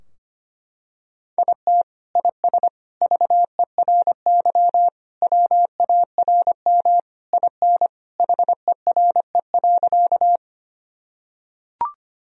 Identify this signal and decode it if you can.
Morse code, heard as a keyed tone, about 25 wpm, 700 Hz. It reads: IT IS VERY WARM IN HERE.
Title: Sentences from Top 1000 Words - Review Part 2 - 25wpm